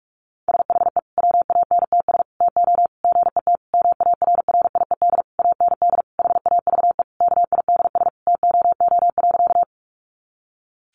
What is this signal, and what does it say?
Morse code, heard as a keyed tone, about 44 wpm, 700 Hz. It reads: SHE WANTS TO GET MARRIED AND HAVE KIDS TOO.
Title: Easy Phrases 45wpm